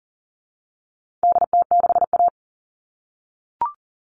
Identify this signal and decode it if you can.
Morse code, heard as a keyed tone, about 40 wpm, 700 Hz. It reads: DT6A
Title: All Cut Numbers - 4-Digit 40wpm